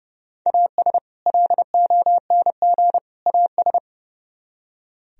Morse code, heard as a keyed tone, about 40 wpm, 700 Hz. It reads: AS LONG AS